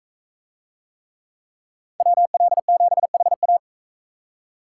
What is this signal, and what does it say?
Morse code, heard as a keyed tone, about 44 wpm, 700 Hz. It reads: WL7HA